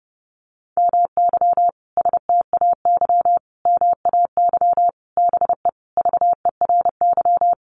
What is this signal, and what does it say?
Morse code, heard as a keyed tone, about 30 wpm, 700 Hz. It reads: MY STAY MAY BE VERY
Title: Sentences from Top 1000 Words 30wpm